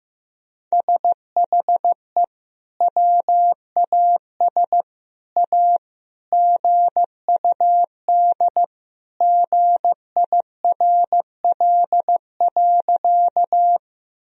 Morse code, heard as a keyed tone, about 15 wpm, 700 Hz. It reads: SHE WAS A GUD GIRL.
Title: Sentences from Top 400 Words and Abbreviations 15wpm